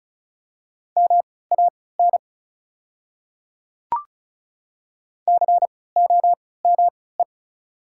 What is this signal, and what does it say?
Morse code, heard as a keyed tone, about 34 wpm, spaced as for 20 wpm, 690 Hz. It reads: MAN COME
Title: 3x Character Spacing - Top 300 Words - 35wpm